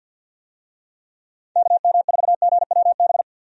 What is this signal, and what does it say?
Morse code, heard as a keyed tone, about 47 wpm, 700 Hz. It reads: KM4GWB
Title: USA Call Signs - 6 Characters - 50wpm